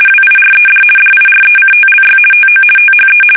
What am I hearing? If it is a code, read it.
RTTY, about 50 baud. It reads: WIRED DIRECTLY TO THE
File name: fsk850-50B / teletype-corp